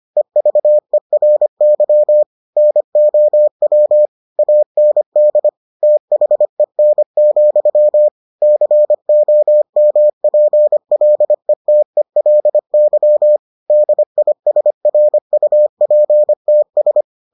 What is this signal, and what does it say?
Morse code, heard as a keyed tone, about 25 wpm, 600 Hz. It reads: EVERY NOW AND THEN, COMPLETELY DISRUPTS